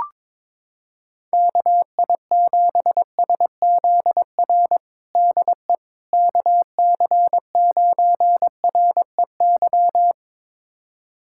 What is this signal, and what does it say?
Morse code, heard as a keyed tone, about 22 wpm, 700 Hz. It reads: KI7SZR DE KC9REY
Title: USA Callsign DE Callsign - 6 Characters - 22wpm